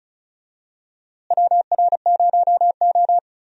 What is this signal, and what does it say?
Morse code, heard as a keyed tone, about 36 wpm, 700 Hz. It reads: WR0O
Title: USA Call Signs - 4 Characters - Repeated 3x - 35wpm